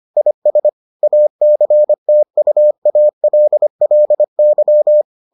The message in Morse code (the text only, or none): IS ACTUALLY